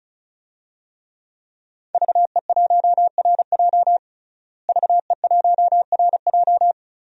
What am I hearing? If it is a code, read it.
Morse code, heard as a keyed tone, about 36 wpm, 700 Hz. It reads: VE1RJ VE1RJ
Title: DXCC-100 Call Signs - 5 Characters - Repeated 3x - 35wpm